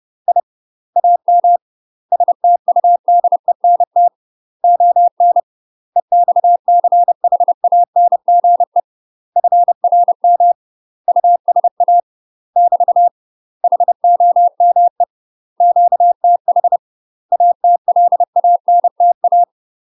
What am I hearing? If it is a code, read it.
Morse code, heard as a keyed tone, about 30 wpm, 700 Hz. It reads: I AM STUDENT ON EXCHANGE FRM USA <BT> HOME QTH ATLANTA
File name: UK MASTER QSOs TRAINING / 30WPM / UK MASTER QSOs 0043